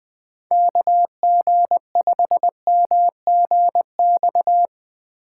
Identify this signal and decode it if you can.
Morse code, heard as a keyed tone, about 20 wpm, 700 Hz. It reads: KG5MGX
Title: USA Call Signs - 6 Characters - 4x Character Spacing - 20wpm